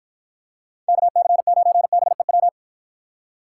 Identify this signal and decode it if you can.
Morse code, heard as a keyed tone, about 51 wpm, 700 Hz. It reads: KC9BEW